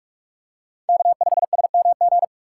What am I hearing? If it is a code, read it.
Morse code, heard as a keyed tone, about 46 wpm, 700 Hz. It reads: K5SMG
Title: USA Call Signs - 5 Characters - 3x Character Spacing - 45wpm